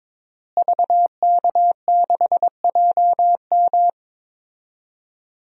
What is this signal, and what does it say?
Morse code, heard as a keyed tone, about 22 wpm, 700 Hz. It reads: VK6JM
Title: DXCC-100 Call Signs - 5 Characters - Repeated 2x - 22wpm